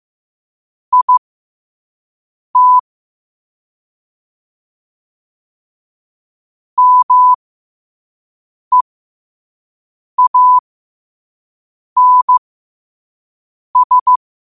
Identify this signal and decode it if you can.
Morse code, heard as a keyed tone, about 15 wpm, spaced as for 5 wpm, 1000 Hz. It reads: IT MEANS
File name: qotdfiles / 5wpm